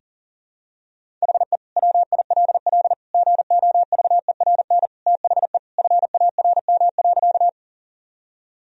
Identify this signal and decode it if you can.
Morse code, heard as a keyed tone, about 40 wpm, 700 Hz. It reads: HE WILL GOVERN THE FARM.
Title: Sentences from Top 1000 Words - Review Part 2 - 40wpm